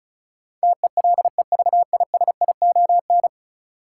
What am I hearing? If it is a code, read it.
Morse code, heard as a keyed tone, about 36 wpm, 700 Hz. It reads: TELEVISION
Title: Sentences from Top 1000 Words - Review Part 4 - New Word Drill - 35wpm